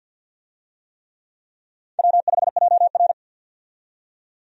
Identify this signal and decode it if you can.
Morse code, heard as a keyed tone, about 52 wpm, 700 Hz. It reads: W5JR
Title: USA Call Signs - 4 Characters - 50wpm